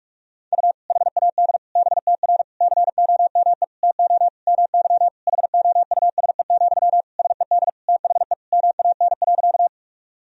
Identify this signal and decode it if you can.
Morse code, heard as a keyed tone, about 46 wpm, 700 Hz. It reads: U HAD BTR COME TO MY HOUSE, SED THE MAN.
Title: Sentences from Top 400 Words and Abbreviations 45wpm